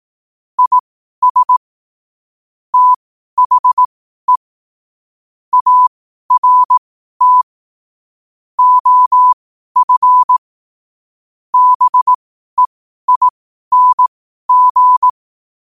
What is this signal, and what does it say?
Morse code, heard as a keyed tone, about 18 wpm, spaced as for 12 wpm, 1000 Hz. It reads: IS THE ART OF BEING
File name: qotdfiles / 12wpm